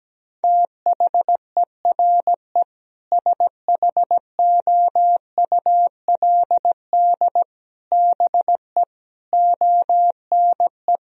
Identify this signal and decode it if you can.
Morse code, heard as a keyed tone, about 17 wpm, 700 Hz. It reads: THERE SHOULD BE ONE
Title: Sentences from Top 1000 Words - Review Part 3 - 17wpm